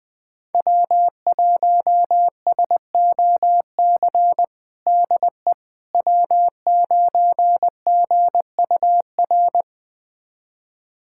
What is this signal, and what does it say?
Morse code, heard as a keyed tone, about 21 wpm, 700 Hz. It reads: W1SOC DE W9GUR
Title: USA Callsign DE Callsign - 5 Characters - 20wpm